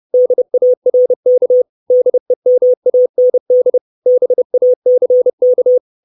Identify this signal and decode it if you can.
Morse code, heard as a keyed tone, about 30 wpm, 500 Hz. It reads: DARK DEMAND BACK
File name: 632 Words 30WPM